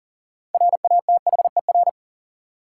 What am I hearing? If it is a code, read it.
Morse code, heard as a keyed tone, about 40 wpm, 700 Hz. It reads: RATHER
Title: Sets of 2 Words - New Word Drill - 40wpm